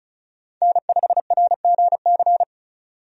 Morse code, heard as a keyed tone, about 36 wpm, 700 Hz. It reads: N5RGC
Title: USA Call Signs - 5 Characters - 35wpm